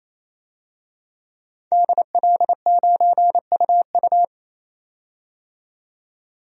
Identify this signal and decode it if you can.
Morse code, heard as a keyed tone, about 28 wpm, 700 Hz. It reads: DL9UU
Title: DXCC-100 Call Signs - 5 Characters - 28wpm